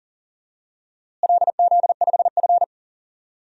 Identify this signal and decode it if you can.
Morse code, heard as a keyed tone, about 40 wpm, 700 Hz. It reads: LZ5F
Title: DXCC-100 Call Signs - 4 Characters - Repeated 3x - 40wpm